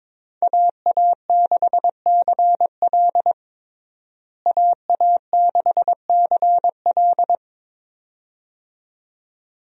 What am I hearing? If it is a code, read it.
Morse code, heard as a keyed tone, about 22 wpm, 700 Hz. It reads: AA6CL AA6CL
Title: USA Call Signs - 5 Characters - Repeated 2x - 22wpm